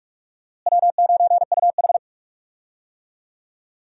Morse code, heard as a keyed tone, about 43 wpm, 700 Hz. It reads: W9UH
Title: USA Call Signs - 4 Characters - Repeated 2x - 45wpm